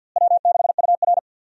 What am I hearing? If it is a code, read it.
Morse code, heard as a keyed tone, about 51 wpm, 700 Hz. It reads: W6UR
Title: USA Call Signs - 4 Characters - Repeated 3x - 50wpm